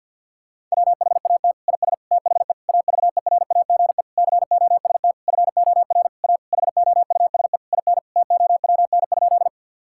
Morse code, heard as a keyed tone, about 50 wpm, 700 Hz. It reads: WHAT IS THE AVERAGE COST FOR A HOUSE IN TOWN?
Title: Sentences from Top 1000 Words - Review Part 6 - 50wpm